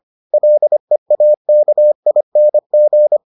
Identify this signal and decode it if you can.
Morse code, heard as a keyed tone, about 25 wpm, 600 Hz. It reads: LEAKING